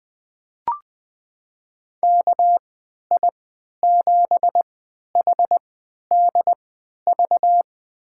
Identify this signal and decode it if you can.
Morse code, heard as a keyed tone, about 20 wpm, spaced as for 11 wpm, 700 Hz. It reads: KI7HDV
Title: USA Call Signs - 6 Characters - 3x Character Spacing - 20wpm